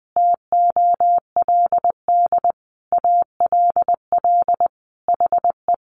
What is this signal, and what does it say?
Morse code, heard as a keyed tone, about 20 wpm, 700 Hz. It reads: TOLD ALL HE
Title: Sentences from Top 400 Words 20wpm